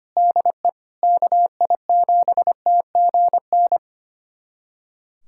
Morse code, heard as a keyed tone, about 25 wpm, 700 Hz. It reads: DE KI7TGN
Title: USA Callsign DE Callsign - 6 Characters - 25wpm